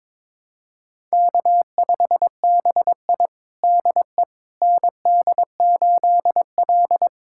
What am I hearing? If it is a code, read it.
Morse code, heard as a keyed tone, about 22 wpm, 700 Hz. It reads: K5BI DE ND8L
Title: USA Callsign DE Callsign - 4 Characters - 22wpm